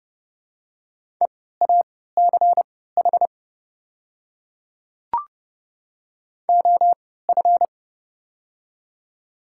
Morse code, heard as a keyed tone, about 30 wpm, spaced as for 17 wpm, 700 Hz. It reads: EACH OF
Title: 3x Character Spacing - Top 100 Words - 30wpm